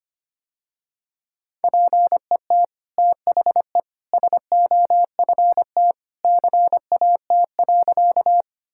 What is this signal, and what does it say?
Morse code, heard as a keyed tone, about 25 wpm, 700 Hz. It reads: PET THE SOFT CAT.
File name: Easy Sentences with Hints - 25wpm